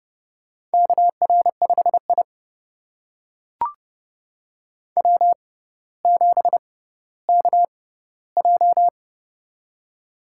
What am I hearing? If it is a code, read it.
Morse code, heard as a keyed tone, about 30 wpm, spaced as for 15 wpm, 700 Hz. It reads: KR5I W7KJ
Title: USA Call Signs - 4 Characters - 6x Character Spacing - 30wpm